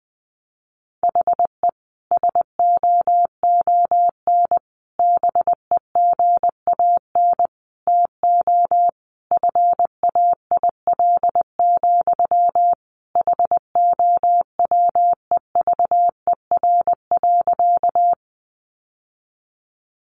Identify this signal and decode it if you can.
Morse code, heard as a keyed tone, about 20 wpm, 700 Hz. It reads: HE SOON BEGAN TO FAIL, HOWEVER.
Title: Sentences from Top 1100 Words 20wpm